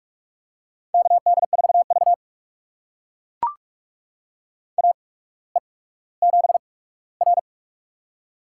Morse code, heard as a keyed tone, about 44 wpm, spaced as for 19 wpm, 700 Hz. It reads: KD4V AE7R
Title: USA Call Signs - 4 Characters - 8x Character Spacing - 45wpm